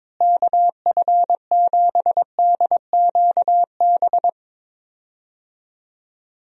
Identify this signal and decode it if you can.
Morse code, heard as a keyed tone, about 22 wpm, 700 Hz. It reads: KF7DQB
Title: USA Call Signs - 6 Characters - Repeated 2x - 22wpm